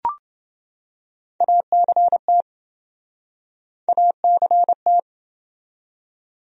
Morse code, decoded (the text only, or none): ACT ACT